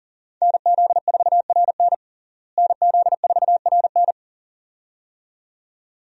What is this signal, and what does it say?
Morse code, heard as a keyed tone, about 40 wpm, 700 Hz. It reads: NZ4RN NZ4RN
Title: USA Call Signs - 5 Characters - Repeated 2x - 40wpm